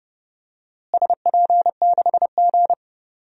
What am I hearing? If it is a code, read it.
Morse code, heard as a keyed tone, about 31 wpm, 700 Hz. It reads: SP6G
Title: DXCC-100 Call Signs - 4 Characters - Repeated 3x - 30wpm